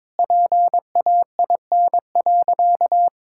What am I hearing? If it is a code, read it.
Morse code, heard as a keyed tone, about 22 wpm, 700 Hz. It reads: PAIN.
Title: Sentences from Top 1600 Words 22wpm